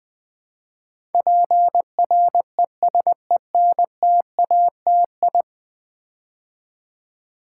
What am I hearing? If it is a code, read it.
Morse code, heard as a keyed tone, about 20 wpm, 700 Hz. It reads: PRESENTATI
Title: Build A Word - 5 Letters At A Time - 12-Letter Words - 20wpm